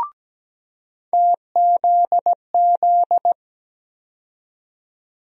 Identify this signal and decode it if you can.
Morse code, heard as a keyed tone, about 17 wpm, 700 Hz. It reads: TZZ
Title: USA Call Sign Postfixes - 3 Letters 17wpm